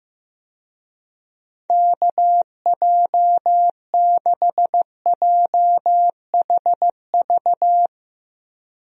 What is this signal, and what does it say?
Morse code, heard as a keyed tone, about 15 wpm, 700 Hz. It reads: KJ6JHV
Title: USA Call Signs - 6 Characters - Repeated 2x - 15wpm